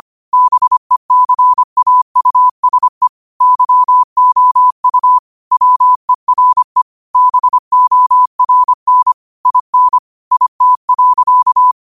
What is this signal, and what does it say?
Morse code, heard as a keyed tone, about 25 wpm, 1000 Hz. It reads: BECAUSE YOU WERE BORN IN IT.
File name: qotdfiles / 25wpm